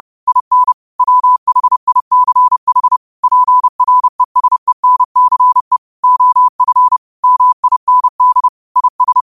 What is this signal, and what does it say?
Morse code, heard as a keyed tone, about 30 wpm, 1000 Hz. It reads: IN WHICH PRESENCE OF MIND IS